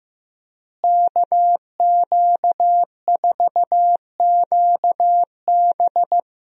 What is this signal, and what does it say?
Morse code, heard as a keyed tone, about 14 wpm, 700 Hz. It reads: KQ4QB
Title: USA Call Signs - 5 Characters - Repeated 2x - 15wpm